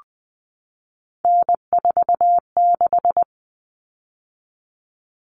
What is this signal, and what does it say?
Morse code, heard as a keyed tone, about 20 wpm, 700 Hz. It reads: N46
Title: Common Cut Numbers - 3-Digit 20wpm